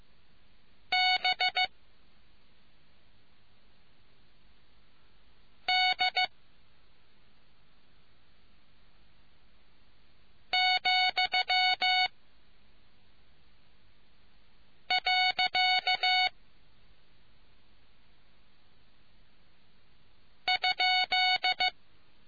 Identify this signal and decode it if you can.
Morse code, heard as a keyed tone, about 15 wpm, 740 Hz. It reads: B D , . ?